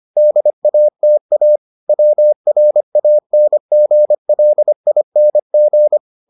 Morse code, heard as a keyed tone, about 25 wpm, 600 Hz. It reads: DATA WRANGLING